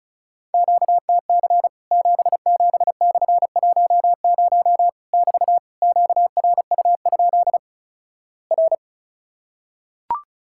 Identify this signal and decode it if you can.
Morse code, heard as a keyed tone, about 36 wpm, 700 Hz. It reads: QTC 77/10 <BT> QRU? R
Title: Worked All Europe DX Contest - QTC Exchange - 35wpm